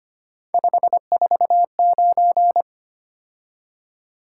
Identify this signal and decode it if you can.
Morse code, heard as a keyed tone, about 25 wpm, 700 Hz. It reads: 549
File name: Three Letter-Number Combinations - 25wpm